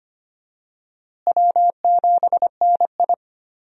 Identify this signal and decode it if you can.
Morse code, heard as a keyed tone, about 25 wpm, 700 Hz. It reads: W7NI
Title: USA Call Signs - 4 Characters - Repeated 2x - 25wpm